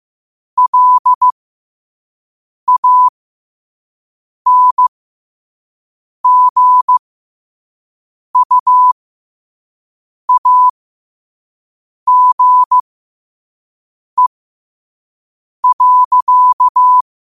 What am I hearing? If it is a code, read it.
Morse code, heard as a keyed tone, about 15 wpm, spaced as for 5 wpm, 1000 Hz. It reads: LANGUAGE.
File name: 5wpm